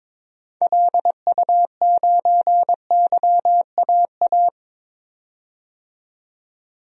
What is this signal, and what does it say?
Morse code, heard as a keyed tone, about 22 wpm, 700 Hz. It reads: LU9YAA
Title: DXCC-100 Call Signs - 6 Characters - 22wpm